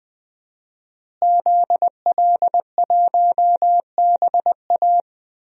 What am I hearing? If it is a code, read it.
Morse code, heard as a keyed tone, about 20 wpm, 700 Hz. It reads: ZL1BA
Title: DXCC-100 Call Signs - 5 Characters - Repeated 2x - 20wpm